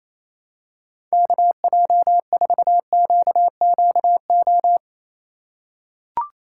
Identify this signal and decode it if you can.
Morse code, heard as a keyed tone, about 28 wpm, 700 Hz. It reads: KJ4QQO
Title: USA Call Signs - 6 Characters - 2x Character Spacing - 28wpm